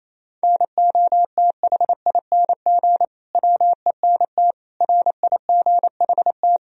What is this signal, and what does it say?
Morse code, heard as a keyed tone, about 28 wpm, 700 Hz. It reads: NOTHING WENT RIGHT
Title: Sentences from Top 1000 Words - Review Part 2 - 28wpm